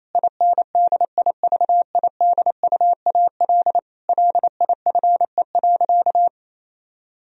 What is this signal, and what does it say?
Morse code, heard as a keyed tone, about 28 wpm, 700 Hz. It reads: INDIVIDUAL LIFE.
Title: Sentences from Top 1000 Words - Review Part 4 - 28wpm